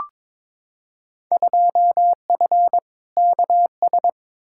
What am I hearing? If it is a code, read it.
Morse code, heard as a keyed tone, about 22 wpm, 700 Hz. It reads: 2F KS
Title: Field Day Exchanges Encore - 22wpm